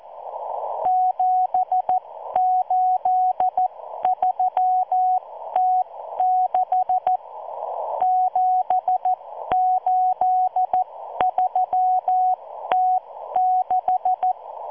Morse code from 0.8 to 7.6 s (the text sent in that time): 783T6